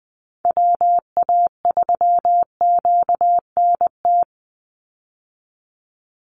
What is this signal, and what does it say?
Morse code, heard as a keyed tone, about 20 wpm, 700 Hz. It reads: WA3QNT
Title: Call Signs Repeated 2x - 20wpm